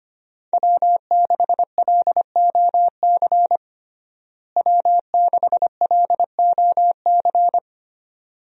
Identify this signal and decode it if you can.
Morse code, heard as a keyed tone, about 25 wpm, 700 Hz. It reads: W6LOC W6LOC
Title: USA Call Signs - 5 Characters - Repeated 2x - 25wpm